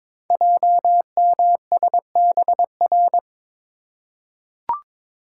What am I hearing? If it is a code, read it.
Morse code, heard as a keyed tone, about 22 wpm, 700 Hz. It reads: JMSBR
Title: Code Groups - 5 Letters - 22wpm